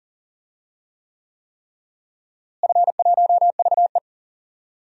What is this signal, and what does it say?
Morse code, heard as a keyed tone, about 40 wpm, 700 Hz. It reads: F1VE
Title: DXCC-100 Call Signs - 4 Characters - Repeated 3x - 40wpm